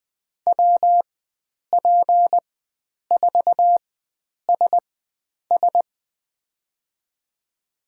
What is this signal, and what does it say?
Morse code, heard as a keyed tone, about 20 wpm, spaced as for 9 wpm, 700 Hz. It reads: WP4SS
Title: USA Call Signs - 5 Characters - 4x Character Spacing - 20wpm